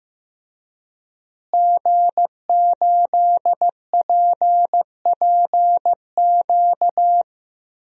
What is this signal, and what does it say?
Morse code, heard as a keyed tone, about 15 wpm, 700 Hz. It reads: G8PPQ
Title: DXCC-100 Call Signs - 5 Characters - Repeated 3x - 15wpm